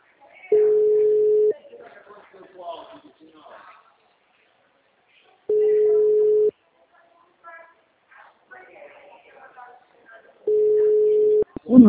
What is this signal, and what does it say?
Ringback tone (Europe, 425 Hz), with speech.
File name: out- -2011- - - .58032